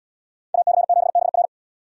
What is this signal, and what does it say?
Morse code, heard as a keyed tone, about 68 wpm, 700 Hz. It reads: AC7BR